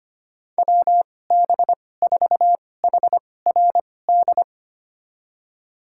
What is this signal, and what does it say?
Morse code, heard as a keyed tone, about 25 wpm, spaced as for 18 wpm, 700 Hz. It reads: WB4HRD
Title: USA Call Signs - 6 Characters - 2x Character Spacing - 25wpm